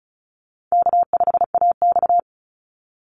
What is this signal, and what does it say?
Morse code, heard as a keyed tone, about 36 wpm, 700 Hz. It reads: K5AX
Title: Call Signs - 35wpm